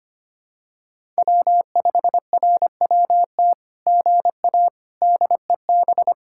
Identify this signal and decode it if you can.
Morse code, heard as a keyed tone, about 24 wpm, 700 Hz. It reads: W5RWT GA DEB
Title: SOTA QSO Practice 25wpm